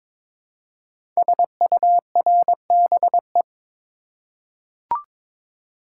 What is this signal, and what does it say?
Morse code, heard as a keyed tone, about 22 wpm, 700 Hz. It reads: SURBE
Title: Code Groups - 5 Letters - 22wpm